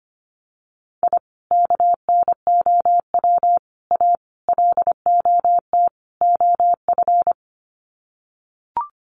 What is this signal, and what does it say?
Morse code, heard as a keyed tone, about 25 wpm, 700 Hz. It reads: I KNOW A LOT OF
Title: Sets of 5 Words Encore 25wpm